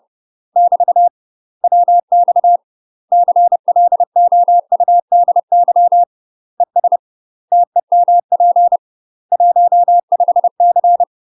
Morse code, heard as a keyed tone, about 30 wpm, 700 Hz. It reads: <BT> WX CLOUDY ES TEMP 15C
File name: UK MASTER QSOs TRAINING / 30WPM / UK MASTER QSOs 0001